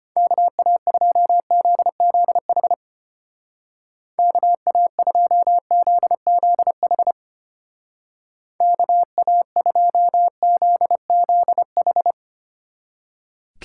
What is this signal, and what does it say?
Morse code, heard as a keyed tone, about 30 wpm, 700 Hz. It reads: KA2ZZH KA2ZZH KA2ZZH
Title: Speed Racing - USA 2x3 Call Signs 25wpm